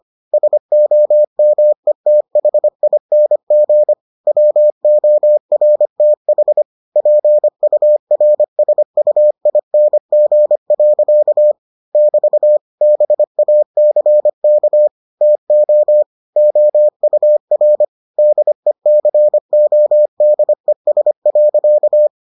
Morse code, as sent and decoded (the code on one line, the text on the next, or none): ... --- -- . - .... .. -. --. / .-- --- .-. - .... / .--. ..- .-. ... ..- .. -. --. .-.-.- / -...- / -... .- -.-. -.- / - --- / --- ..- .-. / -.. -.-. --- -.. ... .-.-.-
SOMETHING WORTH PURSUING. <BT> BACK TO OUR DECODES.